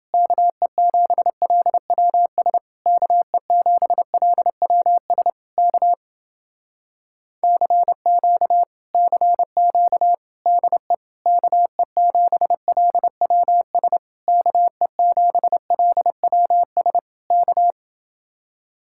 Morse code, heard as a keyed tone, about 29 wpm, 700 Hz. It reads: KE7LWS KE7LWS K CQ CQ DE KE7LWS KE7LWS K